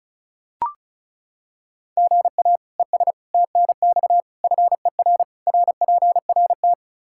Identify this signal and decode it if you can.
Morse code, heard as a keyed tone, about 36 wpm, 700 Hz. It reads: GA ES TNX FER RPRT
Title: Practice QSO - Exchange 1 - Beginning - 35wpm